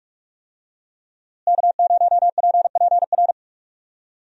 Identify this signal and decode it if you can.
Morse code, heard as a keyed tone, about 46 wpm, 700 Hz. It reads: K0PPR
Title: USA Call Signs - 5 Characters - Repeated 3x - 45wpm